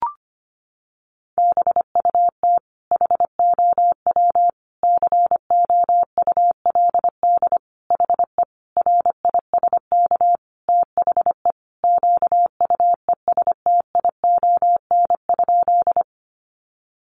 Morse code, heard as a keyed tone, about 25 wpm, 700 Hz. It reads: BUT HOW COULD HE RISK THE QUESTION?
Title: Sentences from Top 600 Words 25wpm